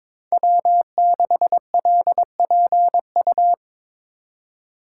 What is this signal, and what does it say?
Morse code, heard as a keyed tone, about 22 wpm, 700 Hz. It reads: W6LPU
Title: USA Call Signs - 5 Characters - 2x Character Spacing - 22wpm